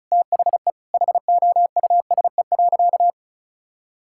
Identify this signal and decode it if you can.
Morse code, heard as a keyed tone, about 36 wpm, 700 Hz. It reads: THE HOUSE.
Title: Sentences from Top 1000 Words - Review Part 6 - 35wpm